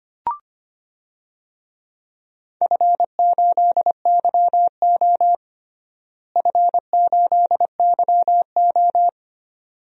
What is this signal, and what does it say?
Morse code, heard as a keyed tone, about 25 wpm, 700 Hz. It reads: F8YO F8YO
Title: DXCC-100 Call Signs - 4 Characters - Repeated 2x - 25wpm